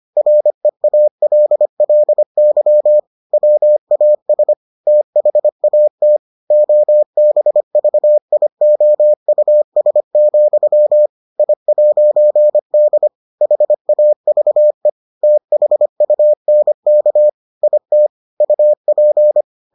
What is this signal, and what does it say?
Morse code, heard as a keyed tone, about 25 wpm, 600 Hz. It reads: REALLY WAS THAT OBVIOUS, I'D HAVE THUNK IT UP